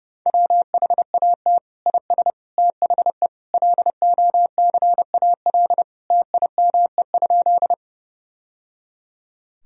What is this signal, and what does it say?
Morse code, heard as a keyed tone, about 30 wpm, 700 Hz. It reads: WHAT IS THE LOCAL TIME?
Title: Sentences from Top 1000 Words - Review Part 3 - 30wpm